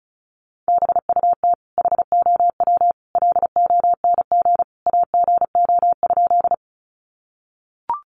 Morse code, heard as a keyed tone, about 34 wpm, 700 Hz. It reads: BUT HOW LONG AGO?
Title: Sentences from Top 500 Words 35wpm